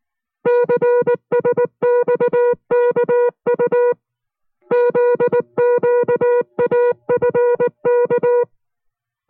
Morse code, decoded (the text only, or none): CSXKU ZQAFK